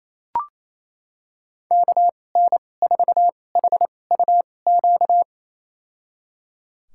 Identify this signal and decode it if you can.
Morse code, heard as a keyed tone, about 28 wpm, spaced as for 20 wpm, 700 Hz. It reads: KN4HUQ
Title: USA Call Signs - 2x Character Spacing - 28wpm